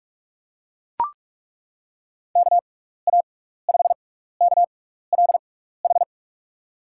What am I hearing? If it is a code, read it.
Morse code, heard as a keyed tone, about 42 wpm, spaced as for 15 wpm, 700 Hz. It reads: KA5KLH
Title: USA Call Signs - 6 Characters - 6x Character Spacing - 45wpm